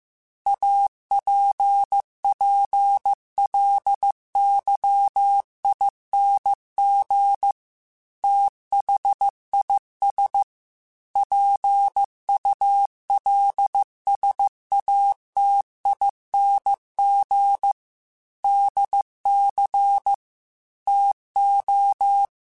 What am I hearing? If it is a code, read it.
Morse code, heard as a keyed tone, about 15 wpm, 800 Hz. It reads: APPLYING THIS PULSATING DC TO
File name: Technical Subjects / Technical Subjects-mp3s / Sw Pwr Sup @13